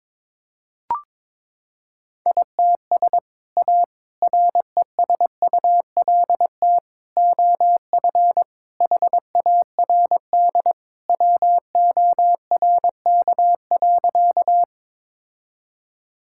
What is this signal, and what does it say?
Morse code, heard as a keyed tone, about 22 wpm, 700 Hz. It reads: ITS A RESULT OF HARD WORK.
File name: practice / 22wpm / Sentences with 3-Word Phrases - 22wpm